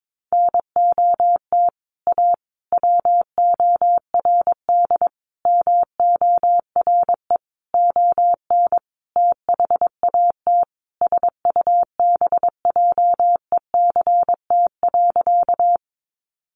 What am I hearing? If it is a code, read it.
Morse code, heard as a keyed tone, about 22 wpm, 700 Hz. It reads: NOT A WORD MORE ON THAT SUBJECT.
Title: Sentences from Top 800 Words 22wpm